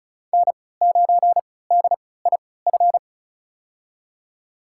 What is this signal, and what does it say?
Morse code, heard as a keyed tone, about 36 wpm, spaced as for 20 wpm, 700 Hz. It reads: N9DIF
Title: USA Call Signs - 5 Characters - 3x Character Spacing - 35wpm